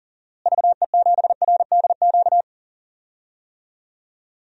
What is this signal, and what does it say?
Morse code, heard as a keyed tone, about 40 wpm, 700 Hz. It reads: VE7RDQ
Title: DXCC-100 Call Signs - 6 Characters - Repeated 3x - 40wpm